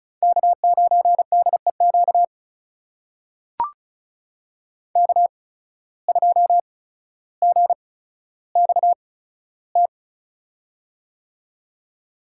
Morse code, heard as a keyed tone, about 36 wpm, spaced as for 15 wpm, 700 Hz. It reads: K9DEQ K2GXT
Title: USA Call Signs - 5 Characters - 8x Character Spacing - 35wpm